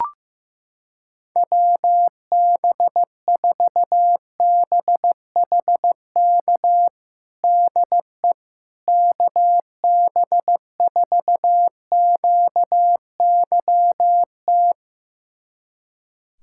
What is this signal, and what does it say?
Morse code, heard as a keyed tone, about 15 wpm, 700 Hz. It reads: WB4BHK DE KB4QYT
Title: USA Callsign DE Callsign - 6 Characters - 15wpm